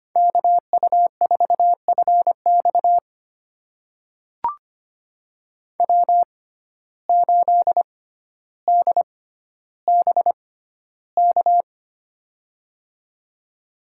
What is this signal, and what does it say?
Morse code, heard as a keyed tone, about 25 wpm, spaced as for 13 wpm, 700 Hz. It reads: KU4FX W8DBK
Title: USA Call Signs - 5 Characters - 6x Character Spacing - 25wpm